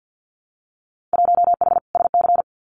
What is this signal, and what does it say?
Morse code, heard as a keyed tone, about 50 wpm, 700 Hz. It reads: 1H SC
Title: Winter Field Day Exchanges 50wpm